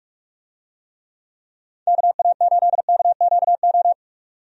Morse code, heard as a keyed tone, about 44 wpm, 700 Hz. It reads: KA8KQO
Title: USA Call Signs - 6 Characters - 45wpm